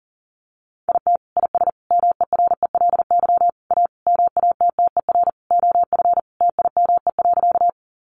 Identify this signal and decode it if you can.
Morse code, heard as a keyed tone, about 40 wpm, 700 Hz. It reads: IT IS MERELY A MATTER OF TIME.
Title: Sentences from Top 1400 Words 40wpm